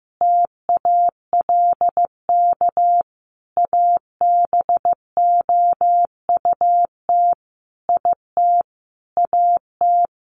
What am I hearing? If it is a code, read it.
Morse code, heard as a keyed tone, about 15 wpm, 700 Hz. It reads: TALK ABOUT IT AT
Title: Sentences from Top 800 Words 15wpm